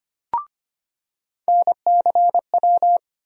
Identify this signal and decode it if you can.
Morse code, heard as a keyed tone, about 25 wpm, 700 Hz. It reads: NCW
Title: USA Call Sign Postfixes - 3 Letters 25wpm